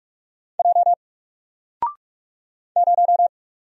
Morse code, heard as a keyed tone, about 44 wpm, 700 Hz. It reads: J 0
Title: Single Letter-Number Alt 45wpm